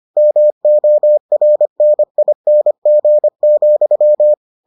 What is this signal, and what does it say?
Morse code, heard as a keyed tone, about 25 wpm, 600 Hz. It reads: MORNING,